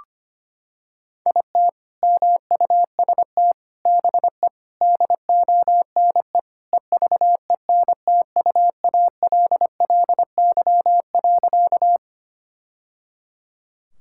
Morse code, heard as a keyed tone, about 25 wpm, 700 Hz. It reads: IT MUST BE DONE EVENTUALLY.
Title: Sentences from Top 1000 Words - Review Part 6 - 25wpm